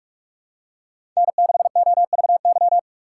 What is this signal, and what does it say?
Morse code, heard as a keyed tone, about 46 wpm, 700 Hz. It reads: N6OVY
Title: USA Call Signs - 5 Characters - 3x Character Spacing - 45wpm